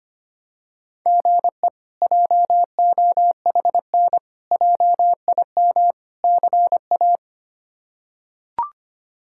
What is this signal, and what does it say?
Morse code, heard as a keyed tone, about 25 wpm, 700 Hz. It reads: GE JOHN JIM CA
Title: K1USN SST QSO Practice - 25wpm